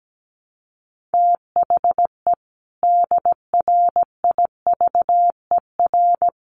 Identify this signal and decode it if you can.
Morse code, heard as a keyed tone, about 17 wpm, 700 Hz. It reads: THE DRIVER
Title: Sentences from Top 1000 Words 17wpm